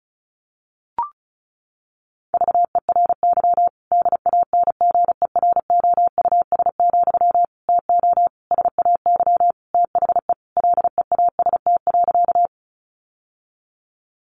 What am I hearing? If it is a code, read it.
Morse code, heard as a keyed tone, about 34 wpm, 700 Hz. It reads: VERY DANGEROUS, TO SAY THE LEAST.
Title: Sentences from Top 1200 Words 35wpm